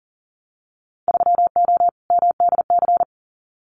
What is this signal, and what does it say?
Morse code, heard as a keyed tone, about 40 wpm, 700 Hz. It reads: 3O MDC
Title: Winter Field Day Exchanges Encore 40wpm